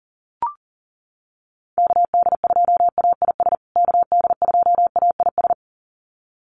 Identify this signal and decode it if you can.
Morse code, heard as a keyed tone, about 40 wpm, 700 Hz. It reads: KD2AIS KD2AIS
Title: USA Call Signs Repeated 2x 40wpm